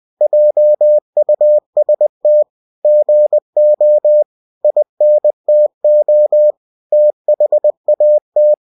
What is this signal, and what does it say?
Morse code, heard as a keyed tone, about 20 wpm, 600 Hz. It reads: JUST GO INTO THAT
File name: cw-books / winnie-the-pooh / winnie-20wpm-page-0057